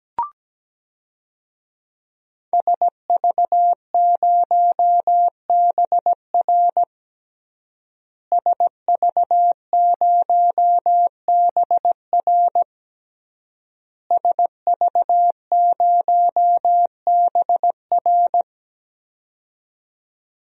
Morse code, heard as a keyed tone, about 17 wpm, 700 Hz. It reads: SV0BR SV0BR SV0BR
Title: DXCC-100 Call Signs - 5 Characters - Repeated 3x - 17wpm